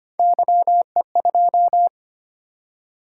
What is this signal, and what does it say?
Morse code, heard as a keyed tone, about 25 wpm, 700 Hz. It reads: YE2
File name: Three Letter-Number Combinations - 25wpm